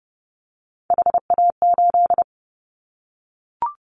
Morse code, heard as a keyed tone, about 28 wpm, 700 Hz. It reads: HA8
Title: Callsign Prefixes for Top 25 DXCC - Repeated 3x - 30wpm